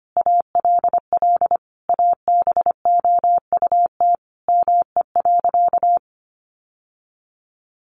Morse code, heard as a keyed tone, about 25 wpm, 700 Hz. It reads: ALL ABOUT ME.
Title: Sentences from Top 1200 Words 25wpm